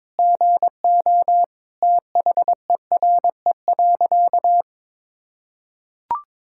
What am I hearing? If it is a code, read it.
Morse code, heard as a keyed tone, about 22 wpm, 700 Hz. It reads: GO THERE.
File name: practice / 22wpm / Sentences with 4-Word Phrases - 22wpm